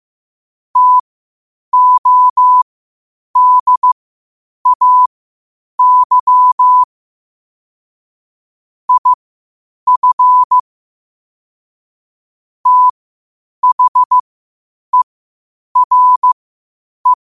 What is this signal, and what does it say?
Morse code, heard as a keyed tone, about 15 wpm, spaced as for 8 wpm, 1000 Hz. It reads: TODAY IF THERE
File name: qotdfiles / 8wpm